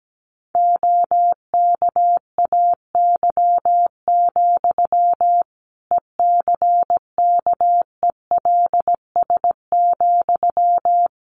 Morse code, heard as a keyed tone, about 17 wpm, 700 Hz. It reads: OKAY, ECKELS,
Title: A Sound Of Thunder 17wpm